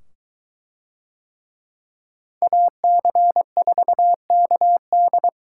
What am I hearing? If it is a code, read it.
Morse code, heard as a keyed tone, about 23 wpm, 700 Hz. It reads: AC4KD